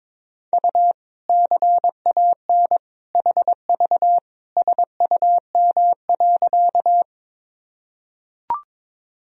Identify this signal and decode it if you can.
Morse code, heard as a keyed tone, about 22 wpm, 700 Hz. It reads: U CAN HV SUM.
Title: Sentences from Top 100 Words and Abbreviations 22wpm